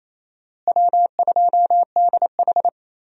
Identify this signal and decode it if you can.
Morse code, heard as a keyed tone, about 28 wpm, 700 Hz. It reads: W2DH
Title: USA Call Signs - 4 Characters - Repeated 3x - 28wpm